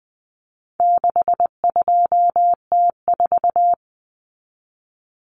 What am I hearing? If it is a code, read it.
Morse code, heard as a keyed tone, about 20 wpm, 700 Hz. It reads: 62T4
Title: Common Cut Numbers - 4-Digit 20wpm